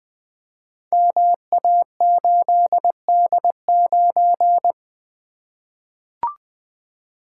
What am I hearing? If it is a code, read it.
Morse code, heard as a keyed tone, about 21 wpm, 700 Hz. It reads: MA8D9
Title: Code Groups - 5 Letter:Numbers - 20wpm